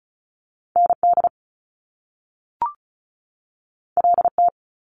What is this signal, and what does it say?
Morse code, heard as a keyed tone, about 36 wpm, 700 Hz. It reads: ND LT
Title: Top Two Letter Combinations in Words 35wpm